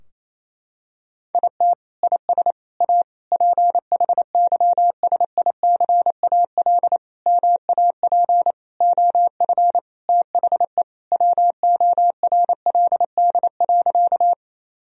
Morse code, heard as a keyed tone, about 28 wpm, 700 Hz. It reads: IT IS A PHYSICAL MAP OF THE WORLD.
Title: Sentences from Top 1000 Words - Review Part 4 - 28wpm